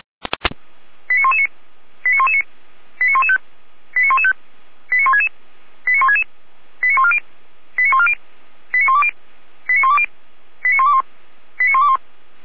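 Five-tone selective call (ZVEI), heard as quick runs of five tones, tones of about 70 ms. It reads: 89119 89119 89115 89115 89166 89166 89140 89140 89120 89120 89121 89121
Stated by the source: ZVEI five-tone selective call 89110 received on SDR.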